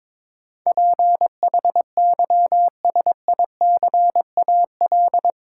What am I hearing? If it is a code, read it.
Morse code, heard as a keyed tone, about 23 wpm, 700 Hz. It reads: PHYSICAL
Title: Sentences from Top 1400 Words - New Word Drill Alt - 22wpm